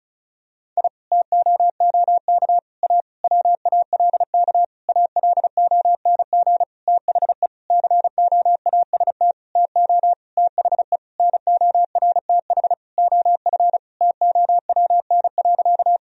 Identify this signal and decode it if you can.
Morse code, heard as a keyed tone, about 36 wpm, 700 Hz. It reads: I TOOK A WALK ALONG THE COAST TO THE NORTH OF TOWN.
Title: Sentences from Top 1000 Words - Review Part 4 - 35wpm